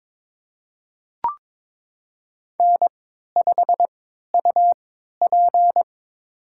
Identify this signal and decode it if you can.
Morse code, heard as a keyed tone, about 22 wpm, spaced as for 13 wpm, 700 Hz. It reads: N5UP
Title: USA Call Signs - 4 Characters - 3x Character Spacing - 22wpm